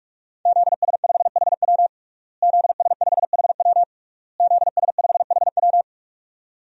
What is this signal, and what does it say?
Morse code, heard as a keyed tone, about 43 wpm, 700 Hz. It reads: ZS5HW ZS5HW ZS5HW
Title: DXCC-100 Call Signs - 5 Characters - Repeated 3x - 45wpm